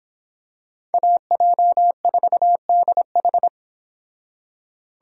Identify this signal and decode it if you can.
Morse code, heard as a keyed tone, about 26 wpm, 700 Hz. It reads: AJ4DH